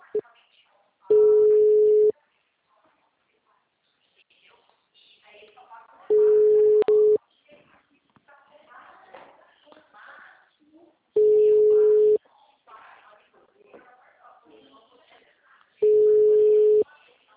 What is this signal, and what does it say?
Ringback tone (Europe, 425 Hz), with speech.